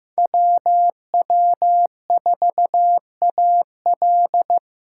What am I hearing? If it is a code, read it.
Morse code, heard as a keyed tone, about 18 wpm, 700 Hz. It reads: WW4AL
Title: Speed Racing - USA 2x2 Call Signs 15wpm